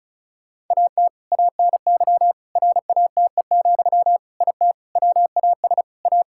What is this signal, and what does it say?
Morse code, heard as a keyed tone, about 36 wpm, 700 Hz. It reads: AT ANY RATE, IT WAS A
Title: Sentences from Top 1600 Words 35wpm